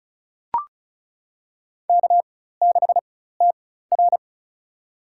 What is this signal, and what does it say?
Morse code, heard as a keyed tone, about 36 wpm, spaced as for 16 wpm, 700 Hz. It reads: K6TR
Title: USA Call Signs - 4 Characters - 4x Character Spacing - 35wpm